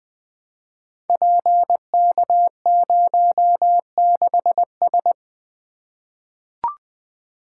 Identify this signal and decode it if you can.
Morse code, heard as a keyed tone, about 20 wpm, 700 Hz. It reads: PK06S
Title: Code Groups - 5 Letter:Numbers - 20wpm